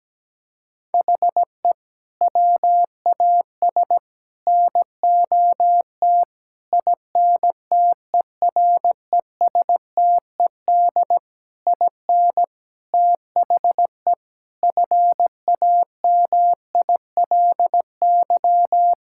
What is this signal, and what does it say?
Morse code, heard as a keyed tone, about 17 wpm, 700 Hz. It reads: HE WAS NOT INTERESTED IN THE FAMILY
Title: Sentences from Top 1000 Words - Review Part 6 - 17wpm